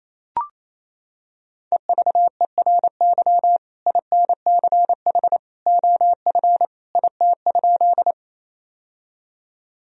Morse code, heard as a keyed tone, about 28 wpm, 700 Hz. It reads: EVERY INCH OF IT?
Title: Sentences from Top 1000 Words - Review Part 3 - 28wpm